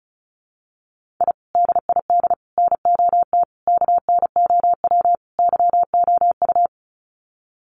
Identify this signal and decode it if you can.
Morse code, heard as a keyed tone, about 34 wpm, 700 Hz. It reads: I DID NOT KNOW YOU
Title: Sets of 5 Words Encore2 35wpm